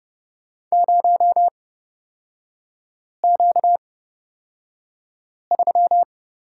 Morse code, heard as a keyed tone, about 29 wpm, spaced as for 5 wpm, 700 Hz. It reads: 0Q3
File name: Lesson 38e - Letter q - Three Characters 30wpm